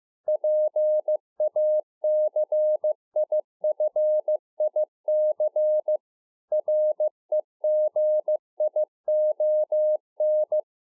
Morse code, heard as a keyed tone, about 15 wpm, 610 Hz. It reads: PACIFIC REGION